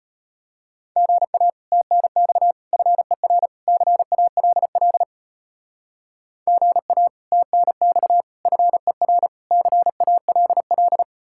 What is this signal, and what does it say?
Morse code, heard as a keyed tone, about 33 wpm, 700 Hz. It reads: GA TNX FER CALL GA TNX FER CALL